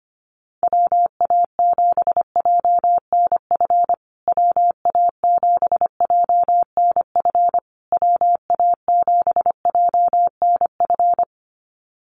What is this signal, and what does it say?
Morse code, heard as a keyed tone, about 25 wpm, 700 Hz. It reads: WA7JNF WA7JNF WA7JNF
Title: USA Call Signs Repeated 3x 25wpm